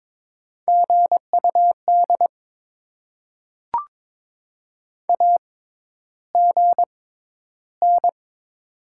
Morse code, heard as a keyed tone, about 22 wpm, spaced as for 11 wpm, 700 Hz. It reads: GUD AGN
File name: Top 20 QSO Elements - 6x Character Spacing - 22wpm